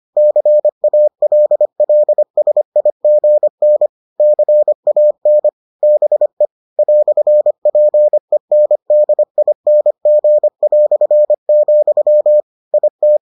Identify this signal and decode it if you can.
Morse code, heard as a keyed tone, about 25 wpm, 600 Hz. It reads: CALLSIGN CAN BE "PENDING", IT